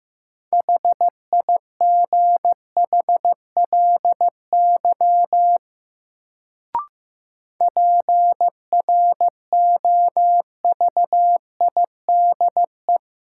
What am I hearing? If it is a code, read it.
Morse code, heard as a keyed tone, about 15 wpm, 700 Hz. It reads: HIGHLY PROVIDE
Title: Sentences from Top 1100 Words - New Word Drill Alt - 15wpm